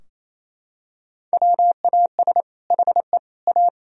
Morse code, heard as a keyed tone, about 28 wpm, 700 Hz. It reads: WAS HE A
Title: Sentences from Top 1000 Words - Review Part 4 - 28wpm